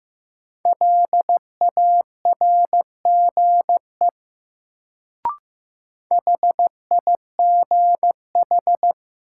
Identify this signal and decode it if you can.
Morse code, heard as a keyed tone, about 15 wpm, 700 Hz. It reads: LARGE HIGH
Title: Sentences from Top 200 Words - New Word Drill - 15wpm